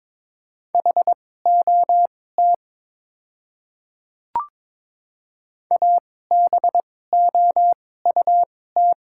Morse code, heard as a keyed tone, about 21 wpm, spaced as for 16 wpm, 700 Hz. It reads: HOT ABOUT
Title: 2x Character Spacing - Top 300 Words - 22wpm